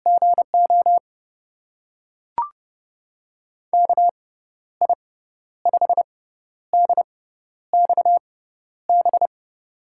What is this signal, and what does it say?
Morse code, heard as a keyed tone, about 30 wpm, spaced as for 12 wpm, 700 Hz. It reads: GO KI5DXB
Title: USA Call Signs - 6 Characters - 6x Character Spacing - 30wpm